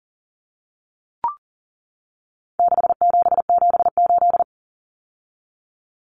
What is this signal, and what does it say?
Morse code, heard as a keyed tone, about 40 wpm, 700 Hz. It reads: 6778
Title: Numbers - 4-Digit 40wpm